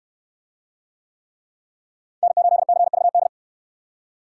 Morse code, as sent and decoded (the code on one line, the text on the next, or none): -. ---.. -.-. ...- -..
N8CVD